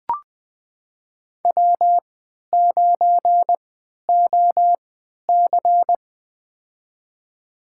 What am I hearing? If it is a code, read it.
Morse code, heard as a keyed tone, about 20 wpm, spaced as for 11 wpm, 700 Hz. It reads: W9OC
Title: USA Call Signs - 4 Characters - 3x Character Spacing - 20wpm